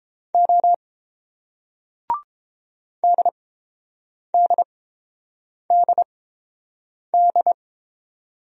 Morse code, heard as a keyed tone, about 30 wpm, spaced as for 7 wpm, 690 Hz. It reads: O DDDD